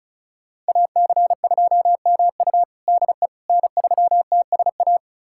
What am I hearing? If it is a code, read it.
Morse code, heard as a keyed tone, about 34 wpm, 700 Hz. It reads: AC2MU DE N3TSA
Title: USA Callsign DE Callsign - 5 Characters - 35wpm